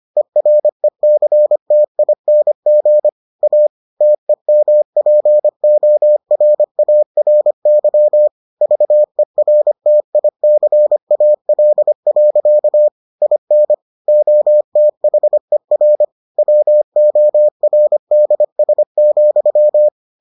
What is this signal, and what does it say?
Morse code, heard as a keyed tone, about 25 wpm, 600 Hz. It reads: ERECTING A TEMPORARY VERTICAL. IN OTHER WORDS,